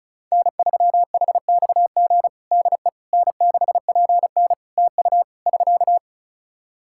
Morse code, heard as a keyed tone, about 36 wpm, 700 Hz. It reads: N3HXG DE N6PN TU <SK>
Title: K1USN SST QSO Practice 35wpm